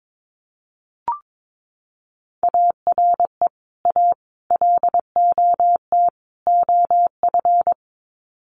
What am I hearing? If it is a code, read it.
Morse code, heard as a keyed tone, about 22 wpm, 700 Hz. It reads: ARE A LOT OF
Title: Sets of 4 Words 22wpm